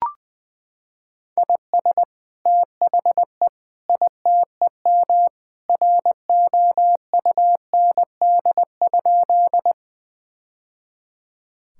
Morse code, heard as a keyed tone, about 20 wpm, 700 Hz. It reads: IS THE ITEM ROUND?
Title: Sentences from Top 1000 Words - Review Part 6 - 20wpm